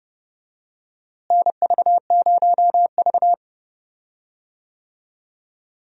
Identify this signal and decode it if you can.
Morse code, heard as a keyed tone, about 32 wpm, 700 Hz. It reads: NV0V
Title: USA Call Signs - 4 Characters - Repeated 2x - 30wpm